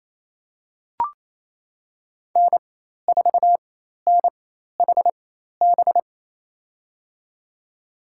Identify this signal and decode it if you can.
Morse code, heard as a keyed tone, about 28 wpm, spaced as for 13 wpm, 700 Hz. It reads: N4NHB
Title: USA Call Signs - 5 Characters - 4x Character Spacing - 28wpm